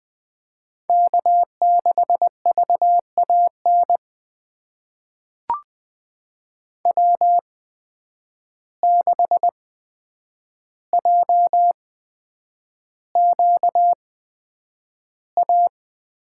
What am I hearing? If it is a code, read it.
Morse code, heard as a keyed tone, about 20 wpm, spaced as for 9 wpm, 700 Hz. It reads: K6VAN W6JQA
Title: USA Call Signs - 5 Characters - 8x Character Spacing - 20wpm